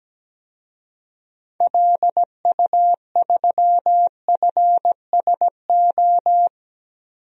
Morse code, heard as a keyed tone, about 17 wpm, 700 Hz. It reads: LU3FSO
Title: DXCC-100 Call Signs - 6 Characters - Repeated 2x - 17wpm